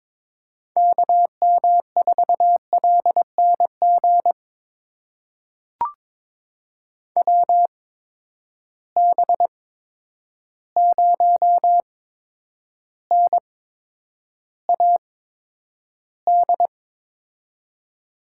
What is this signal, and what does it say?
Morse code, heard as a keyed tone, about 22 wpm, spaced as for 9 wpm, 700 Hz. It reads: KM4LNG WB0NAD
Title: USA Call Signs - 6 Characters - 8x Character Spacing - 22wpm